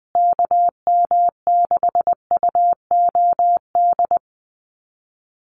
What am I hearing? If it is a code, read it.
Morse code, heard as a keyed tone, about 20 wpm, 700 Hz. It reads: KM6UOD
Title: USA Call Signs Repeated 2x 20wpm